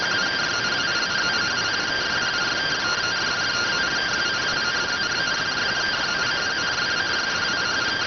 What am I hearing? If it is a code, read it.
Two-tone teleprinter signal, about 170 Hz shift.